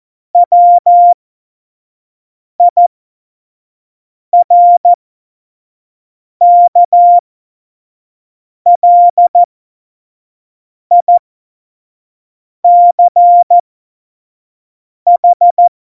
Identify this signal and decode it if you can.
Morse code, heard as a keyed tone, about 14 wpm, spaced as for 5 wpm, 700 Hz. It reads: WIRKLICH
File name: Morsekurs / Witze / Witz 12. September 17, Tempo 25